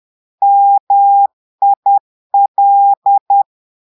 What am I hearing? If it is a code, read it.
Morse code, heard as a keyed tone, about 10 wpm, 800 Hz. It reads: MIL
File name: Portugais - texte en clair 1 - vitesse - 10 mots minute